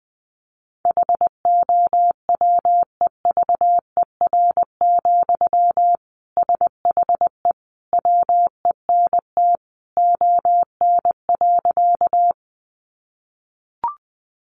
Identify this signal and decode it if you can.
Morse code, heard as a keyed tone, about 20 wpm, 700 Hz. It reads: HOWEVER, SHE WENT ON.
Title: Sentences from Top 600 Words 20wpm